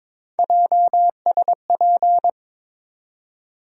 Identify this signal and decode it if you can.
Morse code, heard as a keyed tone, about 22 wpm, 700 Hz. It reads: JSP
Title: USA Call Sign Postfixes - 3 Letters 22wpm